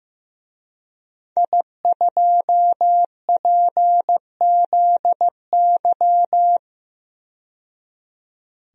Morse code, heard as a keyed tone, about 15 wpm, 700 Hz. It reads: I2PZY
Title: DXCC-100 Call Signs - 5 Characters - 15wpm